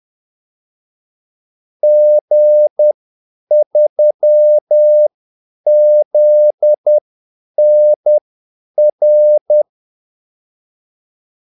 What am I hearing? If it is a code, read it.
Morse code, heard as a keyed tone, about 10 wpm, spaced as for 8 wpm, 600 Hz. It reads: G3ZNR
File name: Callsigns-10-8wpm